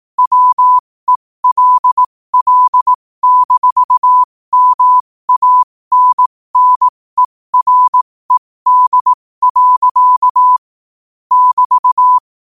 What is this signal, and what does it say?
Morse code, heard as a keyed tone, about 18 wpm, spaced as for 15 wpm, 1000 Hz. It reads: WELL-MANNERED. <BT>